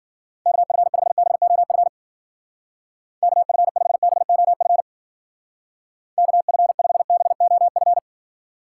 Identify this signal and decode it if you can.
Morse code, heard as a keyed tone, about 50 wpm, 700 Hz. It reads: KF5BOF KF5BOF KF5BOF
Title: Speed Racing - USA 2x3 Call Signs 40wpm